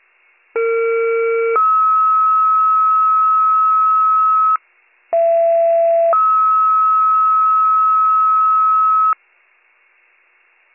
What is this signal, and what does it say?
Two-tone page, heard as a one-second tone, then a three-second tone, about 460 then 1290 Hz, about 670 then 1290 Hz. It is heard from 0.6 to 9.1 s.